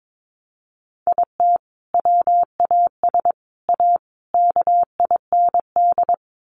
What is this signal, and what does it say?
Morse code, heard as a keyed tone, about 22 wpm, 700 Hz. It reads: IT WAS A KIND
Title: Sentences from Top 500 Words 22wpm